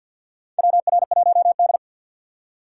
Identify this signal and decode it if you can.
Morse code, heard as a keyed tone, about 52 wpm, 690 Hz. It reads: WR1D
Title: USA Call Signs - 2x Character Spacing - 50wpm